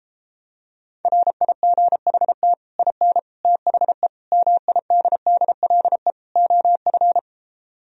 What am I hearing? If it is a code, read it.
Morse code, heard as a keyed tone, about 31 wpm, 700 Hz. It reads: RIGHT IN THE MIDDLE OF